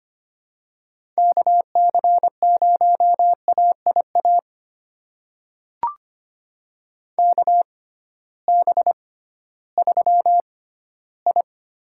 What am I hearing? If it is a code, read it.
Morse code, heard as a keyed tone, about 25 wpm, spaced as for 15 wpm, 700 Hz. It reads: KC0AIA KB3I
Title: USA Call Signs - 6 Characters - 6x Character Spacing - 25wpm